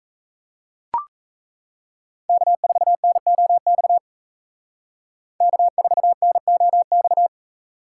Morse code, heard as a keyed tone, about 33 wpm, 700 Hz. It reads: K4NOX K4NOX
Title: Speed Racing - USA Call Signs 28wpm